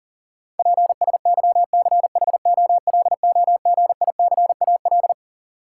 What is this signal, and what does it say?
Morse code, heard as a keyed tone, about 40 wpm, 700 Hz. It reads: PSYCHOLOGICAL
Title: Sentences from Top 1400 Words - New Word Drill - 40wpm